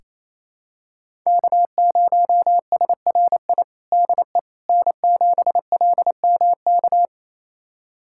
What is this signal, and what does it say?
Morse code, heard as a keyed tone, about 28 wpm, 700 Hz. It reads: K0SRI DE N7LMK
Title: USA Callsign DE Callsign - 5 Characters - 28wpm